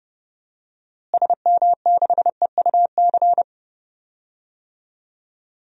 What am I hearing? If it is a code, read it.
Morse code, heard as a keyed tone, about 30 wpm, 700 Hz. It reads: SM6EUC
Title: DXCC-100 Call Signs - 6 Characters - Repeated 2x - 30wpm